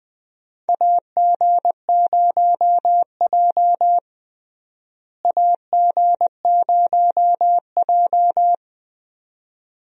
Morse code, heard as a keyed tone, about 20 wpm, 700 Hz. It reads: AG0J AG0J
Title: USA Call Signs - 4 Characters - Repeated 2x - 20wpm